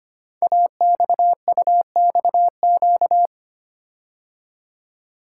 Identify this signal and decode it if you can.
Morse code, heard as a keyed tone, about 25 wpm, 700 Hz. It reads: AXUXQ
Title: Code Groups - 5 Letters - 25wpm